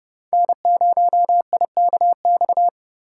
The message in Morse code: N0IKX

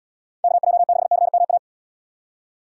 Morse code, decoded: KJ6CMA

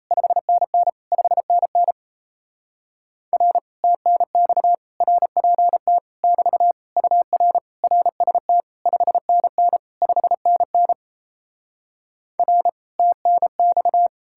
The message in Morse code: 5NN 5NN R TNX RPT <BT> UR RST 5NN 5NN R TNX

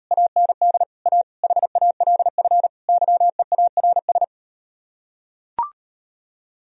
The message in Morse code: AND A HALF YEARS

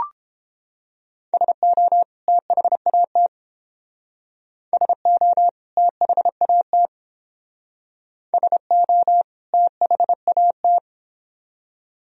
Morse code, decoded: SO THAT SO THAT SO THAT

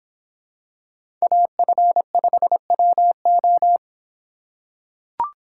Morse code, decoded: AF5WO

